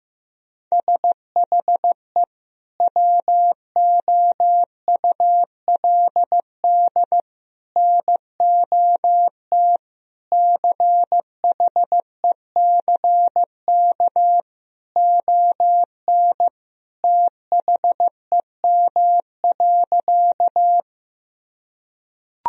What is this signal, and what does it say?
Morse code, heard as a keyed tone, about 15 wpm, 700 Hz. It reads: SHE WOULD NOT CHECK ON THEM.
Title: Sentences from Top 1000 Words - Review Part 3 - 15wpm